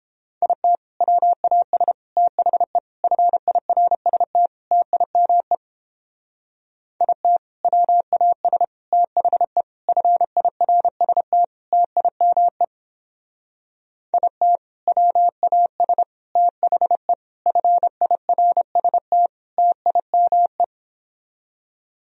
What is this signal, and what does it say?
Morse code, heard as a keyed tone, about 29 wpm, 700 Hz. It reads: IT WAS THE FIRST TIME IT WAS THE FIRST TIME IT WAS THE FIRST TIME